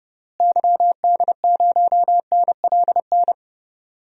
Morse code, heard as a keyed tone, about 30 wpm, 700 Hz. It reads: YD0NLN
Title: DXCC-100 Call Signs - 6 Characters - Repeated 3x - 30wpm